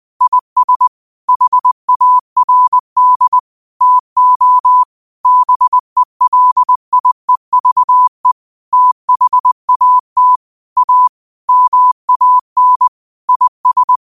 Morse code, heard as a keyed tone, about 20 wpm, 1000 Hz. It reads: IS HARD TO BELIEVE THAT A MAN IS
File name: qotdfiles / 20wpm